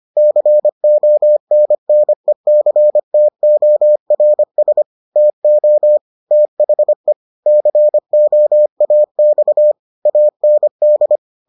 Morse code, heard as a keyed tone, about 25 wpm, 600 Hz. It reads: CONNECTORS TO THE COAX AND